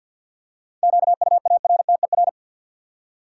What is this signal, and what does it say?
Morse code, heard as a keyed tone, about 53 wpm, 700 Hz. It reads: QUARTER
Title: Seven Letter Words - 50wpm